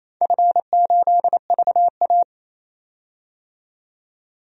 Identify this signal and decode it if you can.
Morse code, heard as a keyed tone, about 27 wpm, 700 Hz. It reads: F8VA
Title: DXCC-100 Call Signs - 4 Characters - 28wpm